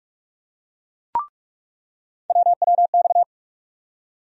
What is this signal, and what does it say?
Morse code, heard as a keyed tone, about 43 wpm, 700 Hz. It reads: WWX
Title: USA Call Sign Postfixes - 3 Letters 45wpm